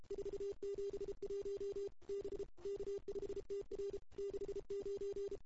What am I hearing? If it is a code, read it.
Morse code, heard as a keyed tone, about 32 wpm, 410 Hz. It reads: 471 B K5TR 69
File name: CW / 2008-ss-cw-k5tr-op-n5rz -01:00